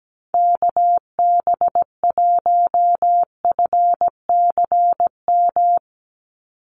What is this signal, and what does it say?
Morse code, heard as a keyed tone, about 17 wpm, 700 Hz. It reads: KB1FCM
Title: USA Call Signs 17wpm